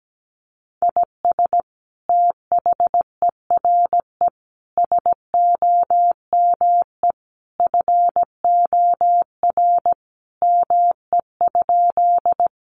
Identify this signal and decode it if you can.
Morse code, heard as a keyed tone, about 17 wpm, 700 Hz. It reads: IS THERE SOME FOR ME?
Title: Sentences from Top 200 Words 17wpm